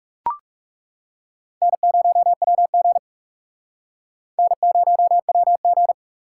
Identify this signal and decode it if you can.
Morse code, heard as a keyed tone, about 40 wpm, 700 Hz. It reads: N0WG N0WG